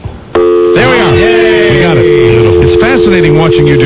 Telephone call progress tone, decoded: dial tone